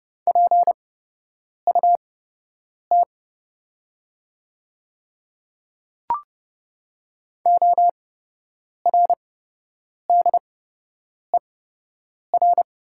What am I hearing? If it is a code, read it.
Morse code, heard as a keyed tone, about 30 wpm, spaced as for 8 wpm, 700 Hz. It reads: PUT ORDER
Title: 8x Character Spacing - Top 500 Words - 30wpm